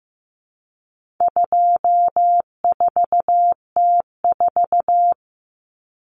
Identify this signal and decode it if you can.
Morse code, heard as a keyed tone, about 15 wpm, 700 Hz. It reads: 24T4
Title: Common Cut Numbers - 4-Digit 15wpm